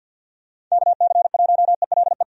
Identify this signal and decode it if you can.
Morse code, heard as a keyed tone, about 47 wpm, 700 Hz. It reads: KC1ELE
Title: USA Call Signs - 6 Characters - 50wpm